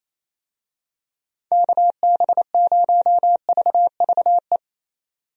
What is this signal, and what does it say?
Morse code, heard as a keyed tone, about 28 wpm, 700 Hz. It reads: KB0VVE
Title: USA Call Signs - 6 Characters - 28wpm